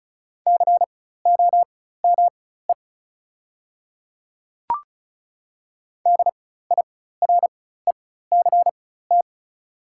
Morse code, heard as a keyed tone, about 36 wpm, spaced as for 16 wpm, 700 Hz. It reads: COME DIRECT
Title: 4x Character Spacing - Top 500 Words - 35wpm